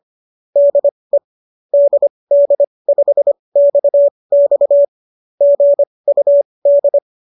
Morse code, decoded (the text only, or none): DE DD5XX GUD